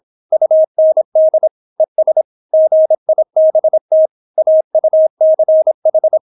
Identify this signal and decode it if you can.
Morse code, heard as a keyed tone, about 26 wpm, 630 Hz. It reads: UND ES GIBT AUCH